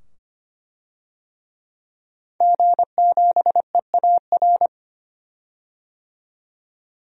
Morse code, heard as a keyed tone, about 25 wpm, 700 Hz. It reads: G7EAR
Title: DXCC-100 Call Signs - 5 Characters - 25wpm